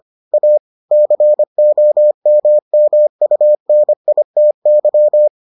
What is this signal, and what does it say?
Morse code, heard as a keyed tone, about 25 wpm, 600 Hz. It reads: A COMMUNITY